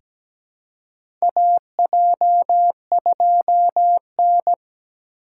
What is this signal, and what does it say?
Morse code, heard as a keyed tone, about 17 wpm, 700 Hz. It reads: AJ2N
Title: USA Call Signs - 4 Characters - 4x Character Spacing - 17wpm